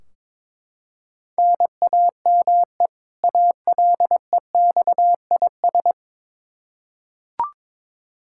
Morse code, heard as a keyed tone, about 22 wpm, 700 Hz. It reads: NAME ALEXIS
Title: Practice QSO - Exchange 1 - Beginning - 22wpm